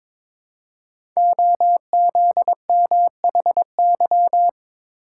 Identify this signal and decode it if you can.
Morse code, heard as a keyed tone, about 22 wpm, 700 Hz. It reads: OZMHY
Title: Code Groups - 5 Letters - 22wpm